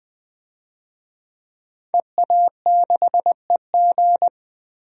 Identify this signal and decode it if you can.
Morse code, heard as a keyed tone, about 20 wpm, 700 Hz. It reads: EA6EG
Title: DXCC-100 Call Signs - 5 Characters - 20wpm